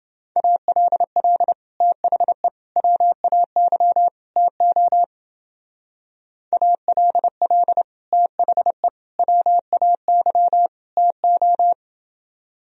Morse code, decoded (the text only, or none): ALL THE WAY TO ALL THE WAY TO